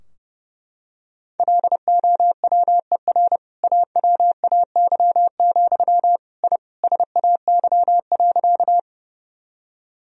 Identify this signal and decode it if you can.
Morse code, heard as a keyed tone, about 30 wpm, 700 Hz. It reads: LOWER AWAY, I SAY.
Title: Sentences from Top 1000 Words - Review Part 4 - 30wpm